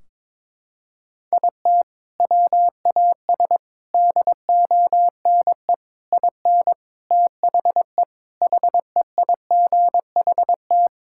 Morse code, heard as a keyed tone, about 22 wpm, 700 Hz. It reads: IT WAS DONE IN THE HEIGHT
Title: Sentences from Top 1600 Words 22wpm